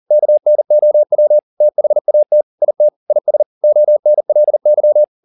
Morse code, heard as a keyed tone, about 40 wpm, 600 Hz. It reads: KNOW THAT IT IS ONLY